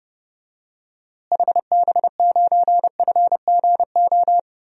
Morse code, heard as a keyed tone, about 30 wpm, 700 Hz. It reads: HB9FGO
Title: DXCC-100 Call Signs - 6 Characters - Repeated 2x - 30wpm